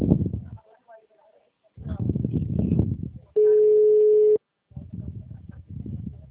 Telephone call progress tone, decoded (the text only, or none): ringback tone